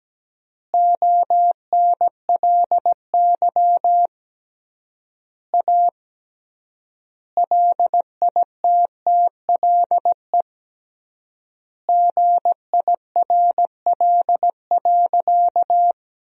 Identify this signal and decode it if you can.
Morse code, heard as a keyed tone, about 17 wpm, 700 Hz. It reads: ONLY A LITTLE GIRL.
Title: 3x Word-Spacing - Sentences from Top 400 Words - 17wpm